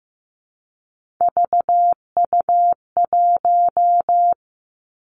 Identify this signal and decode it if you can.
Morse code, heard as a keyed tone, about 15 wpm, 700 Hz. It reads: VU1
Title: Three Challenging Letter-Number Combinations 15wpm